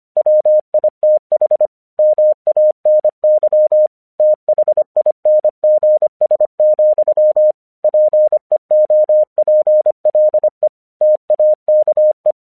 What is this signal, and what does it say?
Morse code, heard as a keyed tone, about 25 wpm, 610 Hz. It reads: WITH MANY THINGS, PEOPLE TAKE
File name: 1 English 25wpm / AQC06